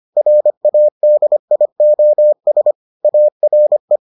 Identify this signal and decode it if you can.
Morse code, heard as a keyed tone, about 25 wpm, 600 Hz. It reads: RADIOS ARE